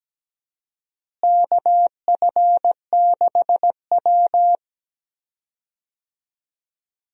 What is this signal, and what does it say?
Morse code, heard as a keyed tone, about 17 wpm, 700 Hz. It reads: KF6W
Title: USA Call Signs - 4 Characters - Repeated 2x - 17wpm